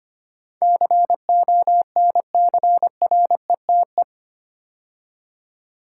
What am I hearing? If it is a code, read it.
Morse code, heard as a keyed tone, about 25 wpm, 700 Hz. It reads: CONCRETE
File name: 8 Letter Words - First 3 Letters Given - 25wpm